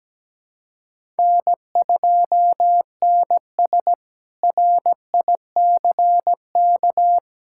Morse code, heard as a keyed tone, about 17 wpm, 700 Hz. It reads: N2NS RICK
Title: K1USN SST QSO Practice - 17wpm